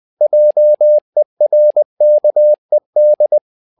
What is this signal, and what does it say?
Morse code, heard as a keyed tone, about 20 wpm, 600 Hz. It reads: JERKED
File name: tower trouble 20wpm